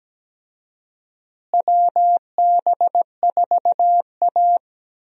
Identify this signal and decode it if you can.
Morse code, heard as a keyed tone, about 17 wpm, 700 Hz. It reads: WB4A
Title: USA Call Signs - 4 Characters - Repeated 3x - 17wpm